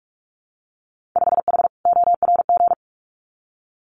Morse code, heard as a keyed tone, about 46 wpm, 690 Hz. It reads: 5H ORG
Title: Winter Field Day Exchanges Encore 45wpm